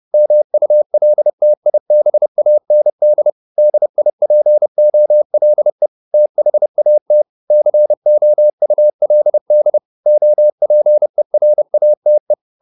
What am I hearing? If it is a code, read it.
Morse code, heard as a keyed tone, about 30 wpm, 600 Hz. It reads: MULTIBAND DIPOLE THAT COULD OPERATE
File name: SS302 30